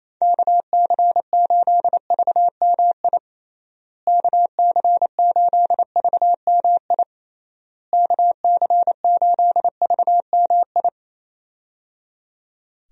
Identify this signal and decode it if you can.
Morse code, heard as a keyed tone, about 28 wpm, 700 Hz. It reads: KC8VMI KC8VMI KC8VMI
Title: USA Call Signs - 6 Characters - Repeated 3x - 28wpm